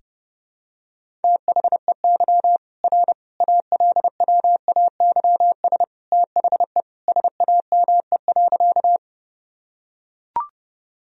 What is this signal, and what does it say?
Morse code, heard as a keyed tone, about 30 wpm, 700 Hz. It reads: THEY R ALWAYS THE SAME.
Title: Sentences from Top 300 Words and Abbreviations 30wpm